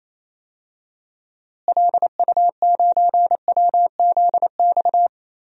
Morse code, heard as a keyed tone, about 27 wpm, 700 Hz. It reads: LU9WZX
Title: DXCC-100 Call Signs - 6 Characters - Repeated 3x - 28wpm